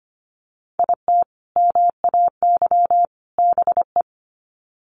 Morse code, heard as a keyed tone, about 25 wpm, 700 Hz. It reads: IT MAY BE
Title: Sets of 3 Words 25wpm